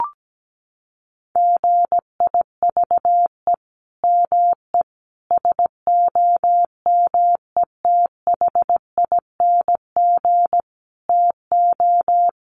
Morse code, heard as a keyed tone, about 17 wpm, 700 Hz. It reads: GIVE ME SOMETHING TO
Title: Sentences from Top 1500 Words 17wpm